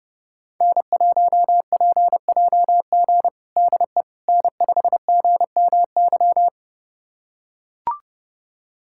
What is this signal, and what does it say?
Morse code, heard as a keyed tone, about 30 wpm, 700 Hz. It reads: N1PJG DE N5GMY
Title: USA Callsign DE Callsign - 5 Characters - 30wpm